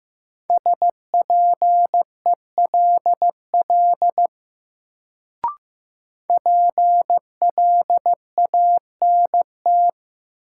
Morse code, heard as a keyed tone, about 15 wpm, 700 Hz. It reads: SPELL PLANT
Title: Sentences from Top 1000 Words - Review Part 1 - New Word Drill - 15wpm